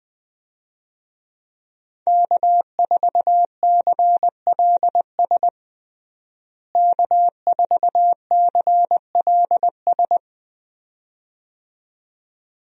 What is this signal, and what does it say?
Morse code, heard as a keyed tone, about 21 wpm, 700 Hz. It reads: K4CLS K4CLS
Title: USA Call Signs - 5 Characters - Repeated 2x - 20wpm